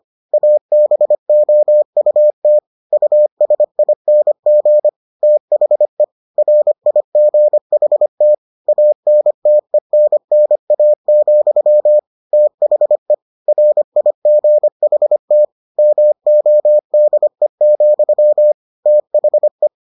Morse code, heard as a keyed tone, about 25 wpm, 600 Hz. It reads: ABOUT USING THE RIGHT ANTENNA, THE RIGHT MODE, THE